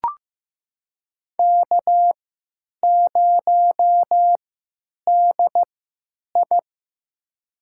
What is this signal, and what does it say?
Morse code, heard as a keyed tone, about 15 wpm, spaced as for 9 wpm, 700 Hz. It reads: K0DI
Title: USA Call Signs - 3x Character Spacing - 15wpm